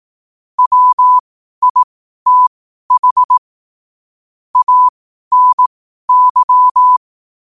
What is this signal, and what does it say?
Morse code, heard as a keyed tone, about 18 wpm, spaced as for 12 wpm, 1000 Hz. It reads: WITH ANY
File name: qotdfiles / 12wpm